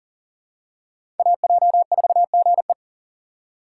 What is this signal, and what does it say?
Morse code, heard as a keyed tone, about 39 wpm, 700 Hz. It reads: AJ4GE
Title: USA Call Signs - 5 Characters - 3x Character Spacing - 40wpm